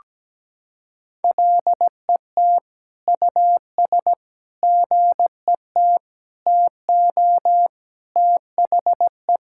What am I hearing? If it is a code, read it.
Morse code, heard as a keyed tone, about 17 wpm, 700 Hz. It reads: LET US GET TO THE
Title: Sentences from Top 1000 Words - Review Part 6 - 17wpm